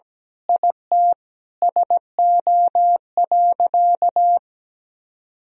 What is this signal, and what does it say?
Morse code, heard as a keyed tone, about 17 wpm, 700 Hz. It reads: IT SO.
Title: Sentences from Top 1000 Words - Review Part 4 - 17wpm